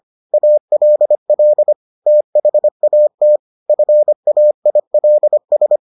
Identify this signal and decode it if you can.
Morse code, heard as a keyed tone, about 25 wpm, 600 Hz. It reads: ALL THAT FAILS